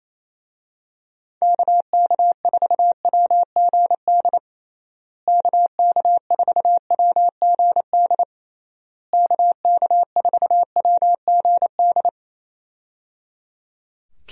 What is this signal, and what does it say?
Morse code, heard as a keyed tone, about 27 wpm, 700 Hz. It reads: KK4WGD KK4WGD KK4WGD
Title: USA Call Signs - 6 Characters - Repeated 3x - 28wpm